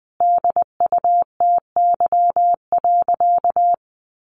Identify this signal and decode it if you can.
Morse code, heard as a keyed tone, about 20 wpm, 700 Hz. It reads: DUTY.
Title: Sentences from Top 1500 Words 20wpm